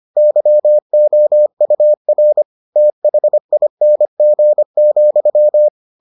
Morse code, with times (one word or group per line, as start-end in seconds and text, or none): 0.2-2.4 YOUR
2.8-5.7 THING,